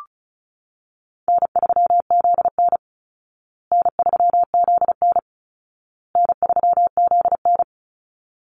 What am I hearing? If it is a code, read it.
Morse code, heard as a keyed tone, about 34 wpm, 700 Hz. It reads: N3ZN N3ZN N3ZN
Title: Easy Call Signs Repeated 3x - 35wpm